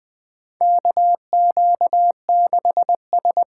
Morse code, heard as a keyed tone, about 20 wpm, 700 Hz. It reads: KQ6S
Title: USA Call Signs - 4 Characters - 3x Character Spacing - 20wpm